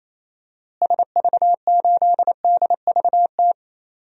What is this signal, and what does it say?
Morse code, heard as a keyed tone, about 28 wpm, 700 Hz. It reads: SV8DVT
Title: DXCC-100 Call Signs - 6 Characters - Repeated 3x - 28wpm